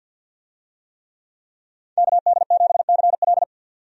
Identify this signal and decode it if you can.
Morse code, heard as a keyed tone, about 47 wpm, 700 Hz. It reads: KD7CL
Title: USA Call Signs - 5 Characters - 50wpm